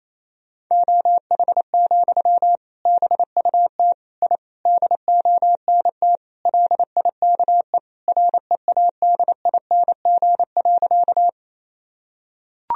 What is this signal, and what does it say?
Morse code, heard as a keyed tone, about 27 wpm, 700 Hz. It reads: OH, BUT I DONT LIKE READING.
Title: Sentences from Top 1000 Words - Review Part 3 - 28wpm